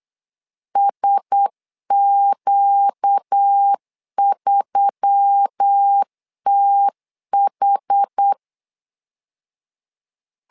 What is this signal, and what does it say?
Morse code, heard as a keyed tone, about 8 wpm, 800 Hz. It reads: SQ3TH
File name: LEZIONI / 5 LBVC / 5 L 8wpm 2831